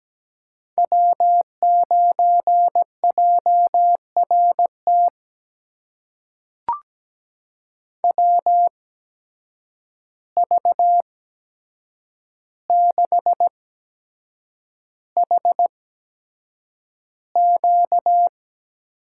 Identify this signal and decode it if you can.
Morse code, heard as a keyed tone, about 17 wpm, spaced as for 7 wpm, 700 Hz. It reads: W9JRT WV6HQ
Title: USA Call Signs - 5 Characters - 8x Character Spacing - 17wpm